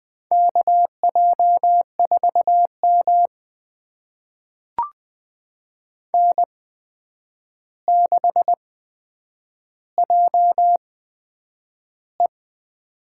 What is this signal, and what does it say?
Morse code, heard as a keyed tone, about 20 wpm, spaced as for 9 wpm, 700 Hz. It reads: KJ4M N6JE
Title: USA Call Signs - 4 Characters - 8x Character Spacing - 20wpm